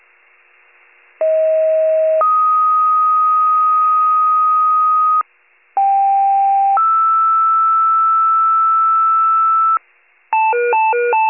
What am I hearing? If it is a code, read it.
Two-tone page, heard as a one-second tone, then a three-second tone, about 640 then 1250 Hz, about 790 then 1320 Hz. It is heard from 1.2 to 9.8 s.
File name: pager / 38 FIRE / wcalert